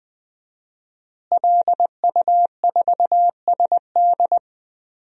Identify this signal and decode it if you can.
Morse code, heard as a keyed tone, about 21 wpm, 700 Hz. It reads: LU4SD
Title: DXCC-100 Call Signs - 5 Characters - Repeated 2x - 20wpm